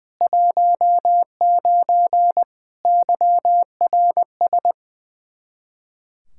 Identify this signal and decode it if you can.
Morse code, heard as a keyed tone, about 20 wpm, 700 Hz. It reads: 19 YRS
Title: Practice QSO - Exchange 3 - More Info - 20wpm